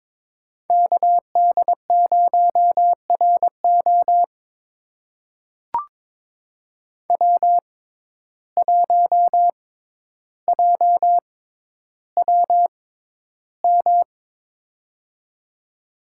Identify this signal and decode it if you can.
Morse code, heard as a keyed tone, about 22 wpm, spaced as for 11 wpm, 700 Hz. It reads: KD0RO W1JWM
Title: USA Call Signs - 5 Characters - 6x Character Spacing - 22wpm